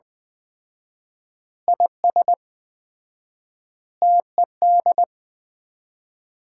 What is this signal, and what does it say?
Morse code, heard as a keyed tone, about 20 wpm, 700 Hz. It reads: IS TED
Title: Common QSO Phrases - 4x Word-Spacing - 20wpm